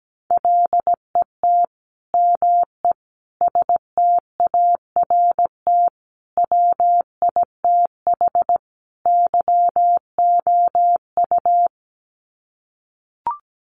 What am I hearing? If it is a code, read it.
Morse code, heard as a keyed tone, about 17 wpm, 700 Hz. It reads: LET ME START WITH YOU
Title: Sets of 5 Words Encore 17wpm